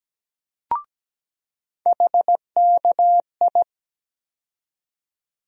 Morse code, heard as a keyed tone, about 17 wpm, 700 Hz. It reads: HKI
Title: USA Call Sign Postfixes - 3 Letters 17wpm